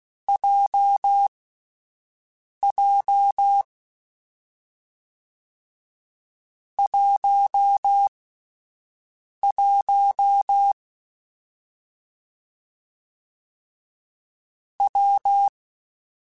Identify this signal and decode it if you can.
Morse code, heard as a keyed tone, about 16 wpm, spaced as for 5 wpm, 800 Hz. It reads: JJ 11 W
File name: Difficult Combos / Difficult Combos-mp3s / WPJ WPJ1 @5